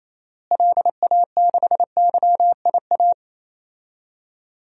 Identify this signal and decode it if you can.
Morse code, heard as a keyed tone, about 28 wpm, 700 Hz. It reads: LA6YIA
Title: DXCC-100 Call Signs - 6 Characters - Repeated 3x - 28wpm